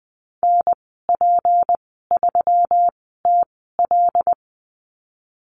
Morse code, heard as a keyed tone, about 20 wpm, spaced as for 15 wpm, 700 Hz. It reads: NP3TL
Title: 186 Lesson 30g - Letter b - Call Signs 20wpm